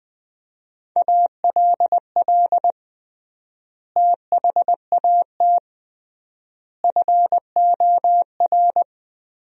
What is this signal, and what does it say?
Morse code, heard as a keyed tone, about 20 wpm, 700 Hz. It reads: ALL THAT FOR
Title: 3x Word-Spacing - Sentences from Top 500 Words - 20wpm